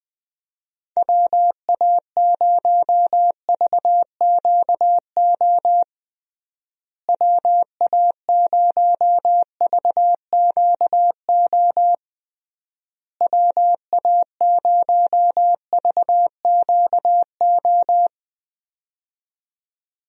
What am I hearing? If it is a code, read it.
Morse code, heard as a keyed tone, about 21 wpm, 700 Hz. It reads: WA0VQO WA0VQO WA0VQO
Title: USA Call Signs - 6 Characters - Repeated 3x - 20wpm